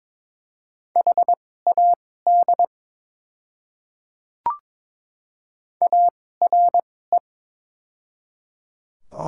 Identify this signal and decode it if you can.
Morse code, heard as a keyed tone, about 22 wpm, spaced as for 16 wpm, 700 Hz. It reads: HAD ARE
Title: 2x Character Spacing - Top 100 Words - 22wpm